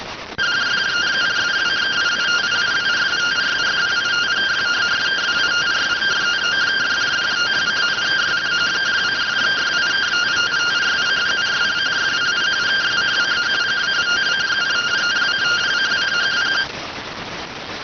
Two-tone teleprinter signal, about 170 Hz shift.